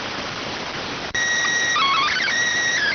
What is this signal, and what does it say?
SSTV (slow-scan television), PD 290.